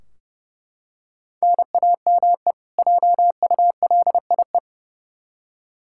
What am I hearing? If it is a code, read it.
Morse code, heard as a keyed tone, about 30 wpm, 700 Hz. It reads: NAME JULIE
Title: K1USN SST QSO Practice 30wpm